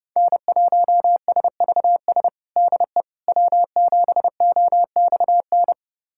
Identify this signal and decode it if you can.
Morse code, heard as a keyed tone, about 32 wpm, 700 Hz. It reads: N1SVS DE W7OXN
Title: USA Callsign DE Callsign - 5 Characters - 30wpm